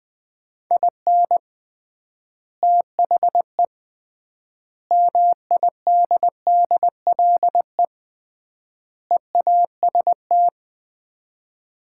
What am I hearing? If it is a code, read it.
Morse code, heard as a keyed tone, about 21 wpm, 700 Hz. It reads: IN THE MIDDLE EAST
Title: 3x Word-Spacing - Sets of 4 Words - 20wpm